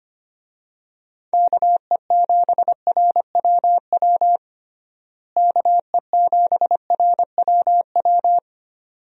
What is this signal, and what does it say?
Morse code, heard as a keyed tone, about 25 wpm, 700 Hz. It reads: KE7RWW KE7RWW
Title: USA Call Signs - 6 Characters - Repeated 2x - 25wpm